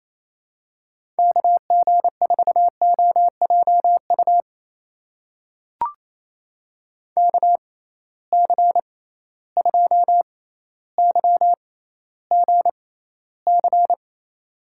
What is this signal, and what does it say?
Morse code, heard as a keyed tone, about 28 wpm, spaced as for 14 wpm, 700 Hz. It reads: KG4OJU KC2YGC
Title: USA Call Signs - 6 Characters - 6x Character Spacing - 28wpm